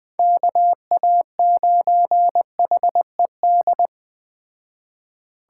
Morse code, heard as a keyed tone, about 20 wpm, 700 Hz. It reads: KA9HED
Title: Speed Racing - USA 2x3 Call Signs 17wpm